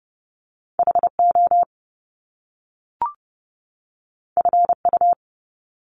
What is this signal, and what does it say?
Morse code, heard as a keyed tone, about 30 wpm, 700 Hz. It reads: HO FU
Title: Top Two Letter Combinations in Words 30wpm